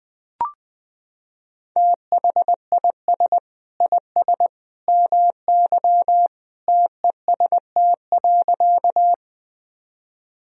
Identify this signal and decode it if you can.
Morse code, heard as a keyed tone, about 21 wpm, 700 Hz. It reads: THIS IS MY TEST.
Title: Sentences from Top 1000 Words - Review Part 3 - 20wpm